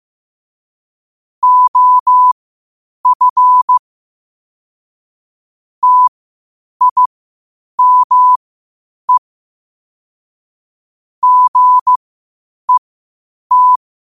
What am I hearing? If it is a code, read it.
Morse code, heard as a keyed tone, about 15 wpm, spaced as for 8 wpm, 1000 Hz. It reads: OF TIME GET
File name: qotdfiles / 8wpm